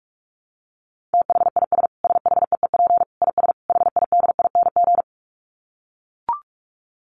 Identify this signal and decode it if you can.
Morse code, heard as a keyed tone, about 44 wpm, 700 Hz. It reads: THIS SHEEP IS HIDING
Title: Dit Challenge 45wpm